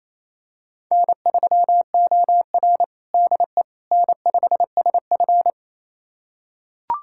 Morse code, heard as a keyed tone, about 28 wpm, 700 Hz. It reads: N3OR DE N5SF
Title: USA Callsign DE Callsign - 4 Characters - 28wpm